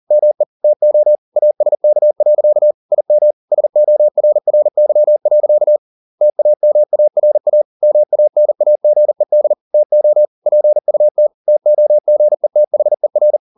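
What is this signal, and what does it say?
Morse code, heard as a keyed tone, about 40 wpm, 600 Hz. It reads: ME TO ASK. IM SORRY. TAMARA MANAGED TO PUT TOGETHER
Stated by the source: Morse code at 40 wpm, practice text, CW med Gnister.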